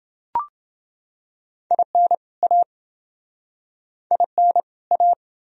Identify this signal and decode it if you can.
Morse code, heard as a keyed tone, about 29 wpm, 700 Hz. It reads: IN A IN A